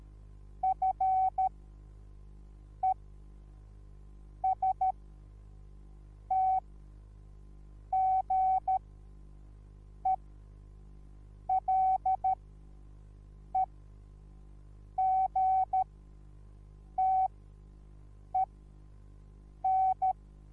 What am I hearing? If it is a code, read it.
Morse code, heard as a keyed tone, about 13 wpm, spaced as for 5 wpm, 760 Hz. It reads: FESTGELEGTEN